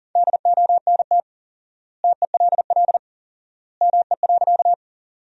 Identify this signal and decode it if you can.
Morse code, heard as a keyed tone, about 40 wpm, 700 Hz. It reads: DONT TELL ME.
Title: 4x Word-Spacing - Sentences from Top 300 Words - 40wpm